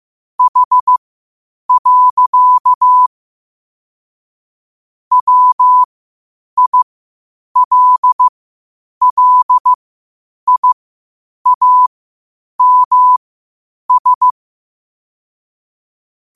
Morse code, heard as a keyed tone, about 15 wpm, spaced as for 8 wpm, 1000 Hz. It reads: H. WILLIAMS
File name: qotdfiles / 8wpm